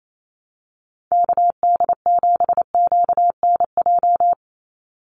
Morse code, heard as a keyed tone, about 28 wpm, 700 Hz. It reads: KD7QNJ
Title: USA Call Signs 28wpm